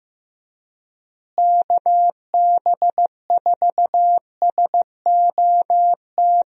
Morse code, heard as a keyed tone, about 15 wpm, 700 Hz. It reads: KB4SOT
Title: USA Call Signs - 6 Characters - Repeated 2x - 15wpm